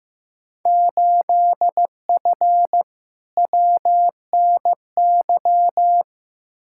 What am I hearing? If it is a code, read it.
Morse code, heard as a keyed tone, about 15 wpm, 700 Hz. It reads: 8F WNY
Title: Field Day Exchanges Encore - 15wpm